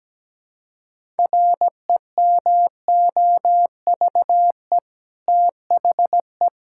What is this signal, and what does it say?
Morse code, heard as a keyed tone, about 17 wpm, 700 Hz. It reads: REMOVE THE
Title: Sentences from Top 1000 Words - Review Part 5 - 17wpm